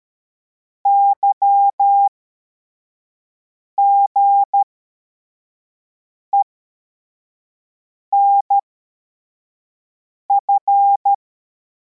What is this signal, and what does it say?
Morse code, heard as a keyed tone, about 13 wpm, spaced as for 4 wpm, 800 Hz. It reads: YGENF